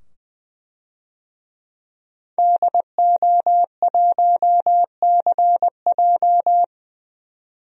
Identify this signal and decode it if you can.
Morse code, heard as a keyed tone, about 21 wpm, 700 Hz. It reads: DO1CJ
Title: DXCC-100 Call Signs - 5 Characters - 20wpm